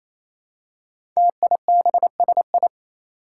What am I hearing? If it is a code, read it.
Morse code, heard as a keyed tone, about 27 wpm, 700 Hz. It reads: TIBSI
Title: Five Challenging Letter-Number Combinations 28wpm